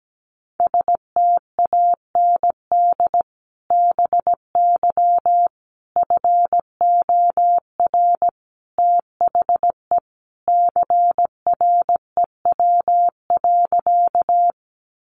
Morse code, heard as a keyed tone, about 17 wpm, 700 Hz. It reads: STAND BY FOR THE CREW.
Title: Sentences from Top 1500 Words 17wpm